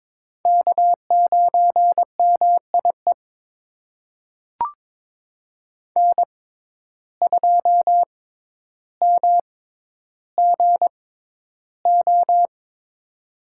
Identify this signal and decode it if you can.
Morse code, heard as a keyed tone, about 23 wpm, spaced as for 11 wpm, 700 Hz. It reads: K9MIE N2MGO
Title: USA Call Signs - 5 Characters - 6x Character Spacing - 22wpm